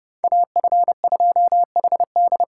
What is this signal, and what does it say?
Morse code, heard as a keyed tone, about 29 wpm, 700 Hz. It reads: AF2HD